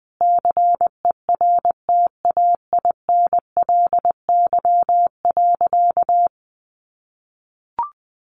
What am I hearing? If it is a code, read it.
Morse code, heard as a keyed tone, about 20 wpm, 700 Hz. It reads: CERTAINLY.
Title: Sentences from Top 900 Words 20wpm